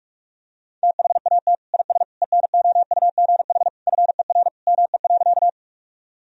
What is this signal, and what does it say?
Morse code, heard as a keyed tone, about 48 wpm, 700 Hz. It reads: THAT IS ENOUGH FER ME.
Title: Sentences from Top 400 Words and Abbreviations 45wpm